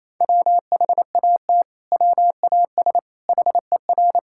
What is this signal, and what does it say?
Morse code, heard as a keyed tone, about 28 wpm, 700 Hz. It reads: WHAT WAS HER
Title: Sentences from Top 1000 Words - Review Part 5 - 28wpm